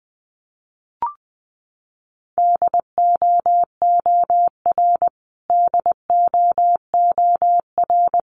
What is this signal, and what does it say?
Morse code, heard as a keyed tone, about 20 wpm, 700 Hz. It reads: DOOR DOOR
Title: Top 500 Words Repeated 20wpm